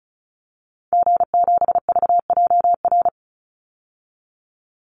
Morse code, heard as a keyed tone, about 34 wpm, 700 Hz. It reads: G7VJR
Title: Call Signs Repeated 2x - 35wpm